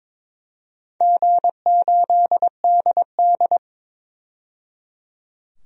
Morse code, heard as a keyed tone, about 22 wpm, 700 Hz. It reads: G8DD
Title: DXCC-100 Call Signs - 4 Characters - 22wpm